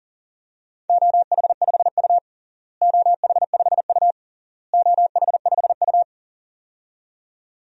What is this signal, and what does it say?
Morse code, heard as a keyed tone, about 40 wpm, 700 Hz. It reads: OH5U OH5U OH5U
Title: DXCC-100 Call Signs - 4 Characters - Repeated 3x - 40wpm